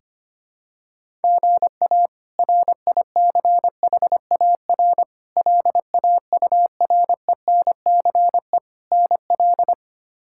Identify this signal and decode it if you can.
Morse code, heard as a keyed tone, about 25 wpm, 700 Hz. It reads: GA RICHAR LAURENCE NL
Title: K1USN SST QSO Practice - 25wpm